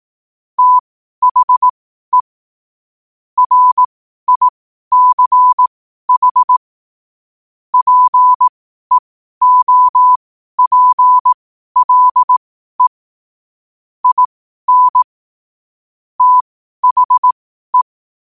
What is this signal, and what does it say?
Morse code, heard as a keyed tone, about 18 wpm, spaced as for 12 wpm, 1000 Hz. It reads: THE RICH PEOPLE IN THE
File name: qotdfiles / 12wpm